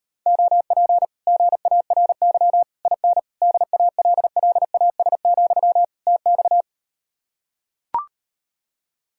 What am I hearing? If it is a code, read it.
Morse code, heard as a keyed tone, about 39 wpm, 700 Hz. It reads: OP GARY IN DALLAS, TX